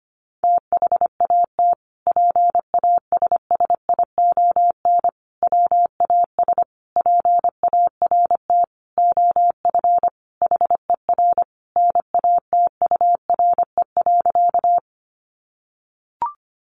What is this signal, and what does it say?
Morse code, heard as a keyed tone, about 25 wpm, 700 Hz. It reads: THAT PASSION WAS PART OF HER NATURE.
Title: Sentences from Top 1300 Words 25wpm